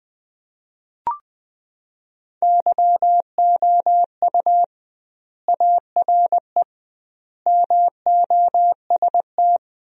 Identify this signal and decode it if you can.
Morse code, heard as a keyed tone, about 20 wpm, 700 Hz. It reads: YOU ARE MOST
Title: 2x Word-Spacing - Sentences from Top 200 Words - 20wpm